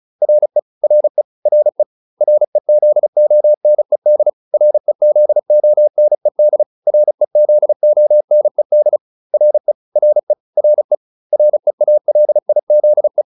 Morse code, heard as a keyed tone, about 34 wpm, 600 Hz. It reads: RE RE RE REZONED REZONED REZONED RE RE RE REALIZE